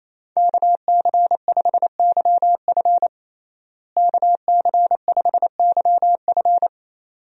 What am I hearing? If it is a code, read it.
Morse code, heard as a keyed tone, about 28 wpm, 700 Hz. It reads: KC5YF KC5YF
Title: USA Call Signs - 5 Characters - Repeated 2x - 28wpm